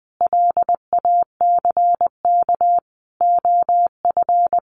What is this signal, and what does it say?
Morse code, heard as a keyed tone, about 20 wpm, 700 Hz. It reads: LACK OF
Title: Sentences from Top 1100 Words 20wpm